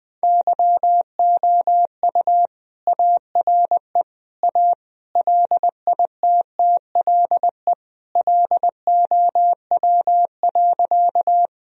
Morse code, heard as a keyed tone, about 20 wpm, 700 Hz. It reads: YOU ARE A LITTLE LOW.
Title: Sentences from Top 1000 Words - Review Part 1 - 20wpm